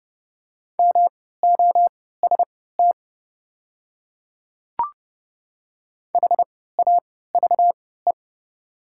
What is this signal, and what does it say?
Morse code, heard as a keyed tone, about 30 wpm, spaced as for 17 wpm, 700 Hz. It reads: MOST HAVE
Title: 3x Character Spacing - Top 100 Words - 30wpm